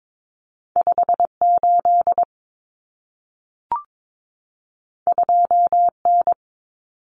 Morse code, heard as a keyed tone, about 22 wpm, 700 Hz. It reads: 58 2N
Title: Common Cut Numbers - 2-Digit 22wpm